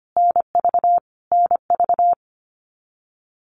Morse code, heard as a keyed tone, about 25 wpm, 700 Hz. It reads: NV NV
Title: US States Repeated 3x 25wpm